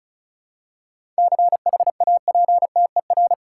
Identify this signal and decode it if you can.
Morse code, heard as a keyed tone, about 34 wpm, 700 Hz. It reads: CHAPTER
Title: Seven Letter Words - 35wpm